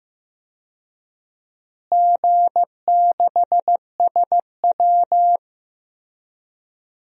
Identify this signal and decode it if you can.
Morse code, heard as a keyed tone, about 15 wpm, 700 Hz. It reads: G6SW
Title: DXCC-100 Call Signs - 4 Characters - Repeated 2x - 15wpm